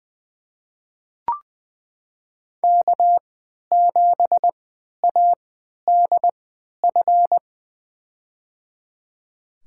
Morse code, heard as a keyed tone, about 20 wpm, spaced as for 11 wpm, 700 Hz. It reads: K7ADF
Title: USA Call Signs - 5 Characters - 3x Character Spacing - 20wpm